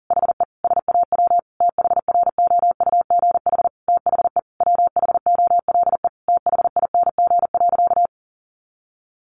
Morse code, heard as a keyed tone, about 40 wpm, 700 Hz. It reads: HE SAW THROUGH THE WHOLE THING.
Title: Sentences from Top 400 Words 40wpm